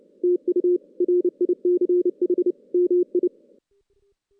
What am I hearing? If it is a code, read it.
Morse code, heard as a keyed tone, about 29 wpm, 360 Hz. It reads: TU RICH MI
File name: 2016 JAN CW NAQP / 2016 Jan naqp k5ot - 0-5